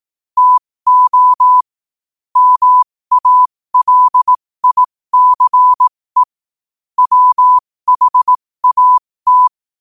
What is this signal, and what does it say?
Morse code, heard as a keyed tone, about 18 wpm, spaced as for 15 wpm, 1000 Hz. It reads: TO MALICE WHAT
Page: New quotes every day in morse code at 15 Words per minute.